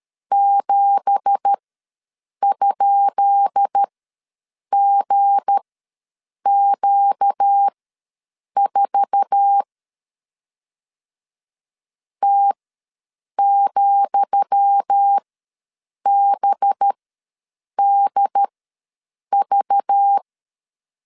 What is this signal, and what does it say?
Morse code, heard as a keyed tone, about 13 wpm, spaced as for 7 wpm, 800 Hz. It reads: 7?GQ4 T,BDV
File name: LEZIONI / 10 SIGNS / 10D COMA ws 12wpm 3618